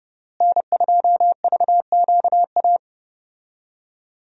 Morse code, heard as a keyed tone, about 29 wpm, 700 Hz. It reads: N2VQA
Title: USA Call Signs - 5 Characters - Repeated 2x - 30wpm